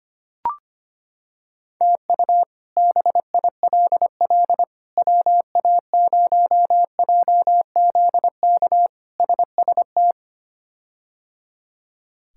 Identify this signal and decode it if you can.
Morse code, heard as a keyed tone, about 25 wpm, 700 Hz. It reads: TU BILL WA0JZK SST
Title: K1USN SST QSO Practice - 25wpm